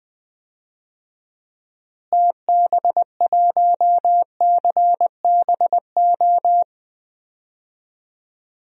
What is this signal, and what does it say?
Morse code, heard as a keyed tone, about 20 wpm, 700 Hz. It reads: TB1CBO
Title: DXCC-100 Call Signs - 6 Characters - Repeated 3x - 20wpm